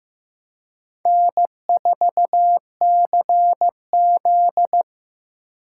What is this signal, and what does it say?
Morse code, heard as a keyed tone, about 15 wpm, 700 Hz. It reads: N4CZ
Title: USA Call Signs - 4 Characters - 3x Character Spacing - 15wpm